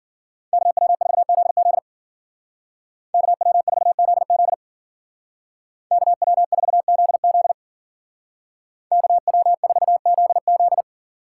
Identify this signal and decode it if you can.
Morse code, heard as a keyed tone, about 50 wpm, 700 Hz. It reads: KW4ZZ KW4ZZ KW4ZZ KW4ZZ
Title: Speed Racing - USA 2x2 Call Signs 40wpm